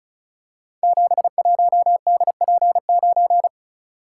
Morse code, heard as a keyed tone, about 36 wpm, 700 Hz. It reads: 71DP9
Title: Code Groups - 5 Letter:Numbers - 35wpm